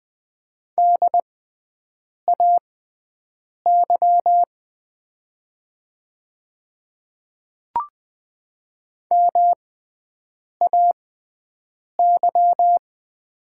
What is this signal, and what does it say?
Morse code, heard as a keyed tone, about 20 wpm, spaced as for 7 wpm, 700 Hz. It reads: DAY MAY
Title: 6x Character Spacing - Top 100 Words - 20wpm